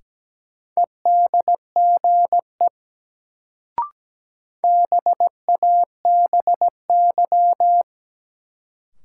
Morse code, heard as a keyed tone, about 17 wpm, 700 Hz. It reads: EDGE BABY
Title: Sentences from Top 1300 Words - New Word Drill - 17wpm